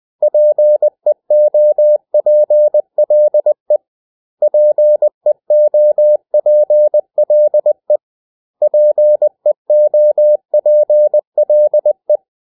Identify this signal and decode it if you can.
Morse code, heard as a keyed tone, about 20 wpm, 600 Hz. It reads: PEOPLE PEOPLE PEOPLE
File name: cw / 100 Common English Words